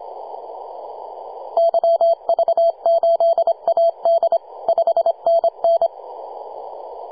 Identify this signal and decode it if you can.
Morse code, heard as a keyed tone, about 25 wpm, 670 Hz. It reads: YV8AD 5NN
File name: posts / 2015-sept-26-2040z-v73a-21-025-cw